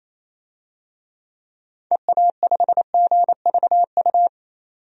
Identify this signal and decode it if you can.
Morse code, heard as a keyed tone, about 28 wpm, 700 Hz. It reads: EA5GVU
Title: DXCC-100 Call Signs - 6 Characters - 28wpm